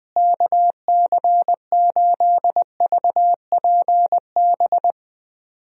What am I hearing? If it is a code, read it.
Morse code, heard as a keyed tone, about 20 wpm, 700 Hz. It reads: KC8VPB
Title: USA Call Signs - 6 Characters - Repeated 2x - 20wpm